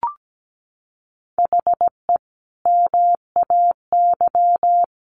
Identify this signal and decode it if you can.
Morse code, heard as a keyed tone, about 17 wpm, 700 Hz. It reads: HE MAY
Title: Sentences from Top 1500 Words 17wpm